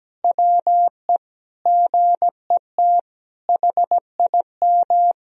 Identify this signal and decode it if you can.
Morse code, heard as a keyed tone, about 17 wpm, 700 Hz. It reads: WE GET HIM
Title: Sentences from Top 1000 Words - Review Part 4 - 17wpm